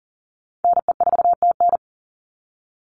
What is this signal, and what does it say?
Morse code, heard as a keyed tone, about 40 wpm, 700 Hz. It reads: NE4TN
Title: NA SOTA Chaser Callsigns 40wpm